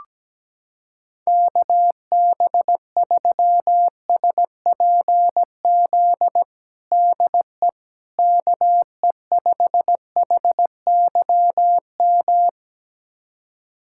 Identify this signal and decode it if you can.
Morse code, heard as a keyed tone, about 17 wpm, 700 Hz. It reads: KB3SPZ DE KE5HYM
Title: USA Callsign DE Callsign - 6 Characters - 17wpm